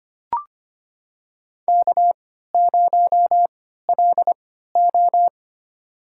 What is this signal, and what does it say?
Morse code, heard as a keyed tone, about 25 wpm, spaced as for 14 wpm, 700 Hz. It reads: K0LO
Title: USA Call Signs - 4 Characters - 3x Character Spacing - 25wpm